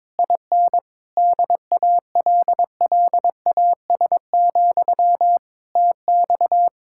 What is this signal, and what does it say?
Morse code, heard as a keyed tone, about 22 wpm, 700 Hz. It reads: IN DALLAS, TX
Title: Speed-Racing Common QSO Phrases - 22wpm